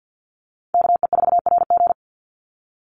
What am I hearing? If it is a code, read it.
Morse code, heard as a keyed tone, about 48 wpm, 700 Hz. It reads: KE4RG
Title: Call Signs - 50wpm